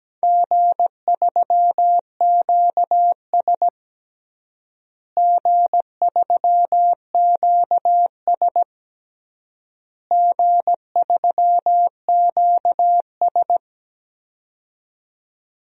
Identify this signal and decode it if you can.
Morse code, heard as a keyed tone, about 17 wpm, 700 Hz. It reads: G3QS G3QS G3QS
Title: DXCC-100 Call Signs - 4 Characters - Repeated 3x - 17wpm